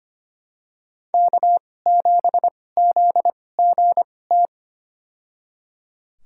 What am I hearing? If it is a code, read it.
Morse code, heard as a keyed tone, about 25 wpm, spaced as for 18 wpm, 700 Hz. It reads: K7ZGT
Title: USA Call Signs - 2x Character Spacing - 25wpm